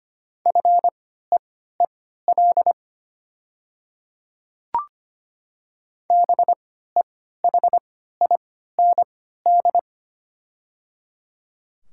Morse code, heard as a keyed tone, about 25 wpm, spaced as for 14 wpm, 690 Hz. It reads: FEEL BEHIND
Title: 3x Character Spacing - Top 500 Words - 25wpm